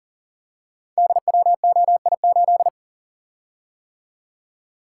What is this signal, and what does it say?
Morse code, heard as a keyed tone, about 42 wpm, 700 Hz. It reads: DWOI8
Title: Code Groups - 5 Letter:Numbers - 40wpm